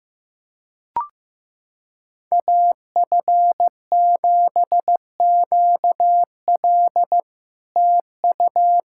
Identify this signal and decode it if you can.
Morse code, heard as a keyed tone, about 15 wpm, 700 Hz. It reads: AF7QL TU
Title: SOTA QSO Practice 15wpm